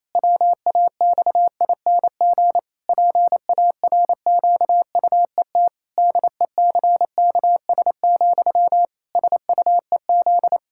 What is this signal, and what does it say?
Morse code, heard as a keyed tone, about 27 wpm, 700 Hz. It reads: WAXING PARQUET DECKS, SUEZ
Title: Pangram Challenge v2 - 28wpm